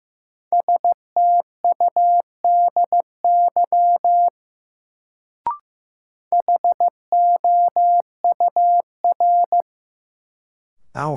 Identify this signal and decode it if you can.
Morse code, heard as a keyed tone, about 15 wpm, 700 Hz. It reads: STUDY HOUR
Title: Sets of 5 Words - New Word Drill - 15wpm